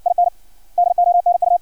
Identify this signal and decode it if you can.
Morse code, heard as a keyed tone, about 56 wpm, 700 Hz. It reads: IN DONU
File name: archived / CRSnet